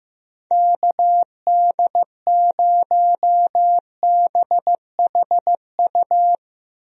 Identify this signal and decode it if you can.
Morse code, heard as a keyed tone, about 15 wpm, 700 Hz. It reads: KD0BHU
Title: USA Call Signs - 6 Characters - Repeated 2x - 15wpm